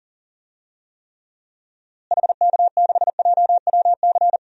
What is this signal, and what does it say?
Morse code, heard as a keyed tone, about 40 wpm, 700 Hz. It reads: HK6JWC
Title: DXCC-100 Call Signs - 6 Characters - Repeated 3x - 40wpm